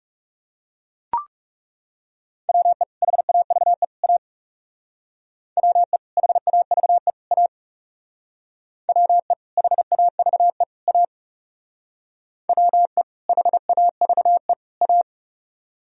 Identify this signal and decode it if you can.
Morse code, heard as a keyed tone, about 35 wpm, 700 Hz. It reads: WE HAVE A WE HAVE A WE HAVE A WE HAVE A
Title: Speed-Racing - Sets of 3 Words 30wpm